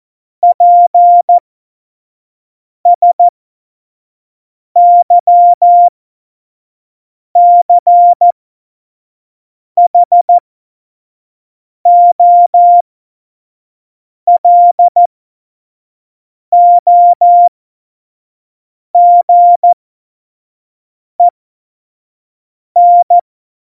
Morse code, heard as a keyed tone, about 14 wpm, spaced as for 5 wpm, 700 Hz. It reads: PSYCHOLOGEN
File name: Morsekurs / Witze / Witz 06 March 2018, 25 F70